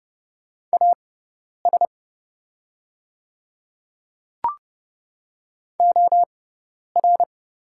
Morse code, heard as a keyed tone, about 30 wpm, spaced as for 10 wpm, 700 Hz. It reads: AS OR
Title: 6x Character Spacing - Top 100 Words - 30wpm